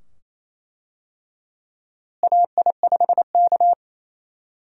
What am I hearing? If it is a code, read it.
Morse code, heard as a keyed tone, about 27 wpm, 700 Hz. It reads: AI5K
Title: USA Call Signs - 4 Characters - Repeated 2x - 28wpm